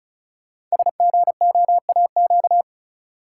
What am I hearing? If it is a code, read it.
Morse code, heard as a keyed tone, about 36 wpm, 700 Hz. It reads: SGOAQ
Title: Code Groups - 5 Letters - 35wpm